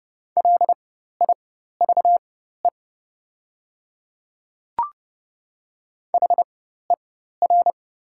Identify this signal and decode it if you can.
Morse code, heard as a keyed tone, about 29 wpm, spaced as for 14 wpm, 690 Hz. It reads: LIVE HER
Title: 4x Character Spacing - Top 200 Words - 30wpm